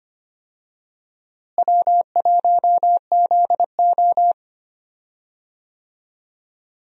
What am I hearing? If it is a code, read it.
Morse code, heard as a keyed tone, about 25 wpm, 700 Hz. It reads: W1ZO
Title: USA Call Signs - 4 Characters - Repeated 2x - 25wpm